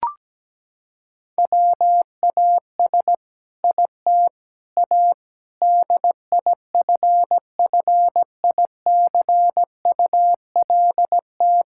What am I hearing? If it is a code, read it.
Morse code, heard as a keyed tone, about 17 wpm, 700 Hz. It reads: WAS IT A DIFFICULT
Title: Sentences from Top 1000 Words - Review Part 6 - 17wpm